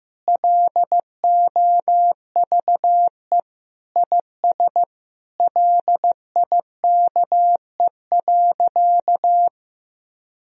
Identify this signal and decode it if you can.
Morse code, heard as a keyed tone, about 16 wpm, 700 Hz. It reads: LOVE IS LIKE.
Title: Sentences from Top 1000 Words - Review Part 3 - 15wpm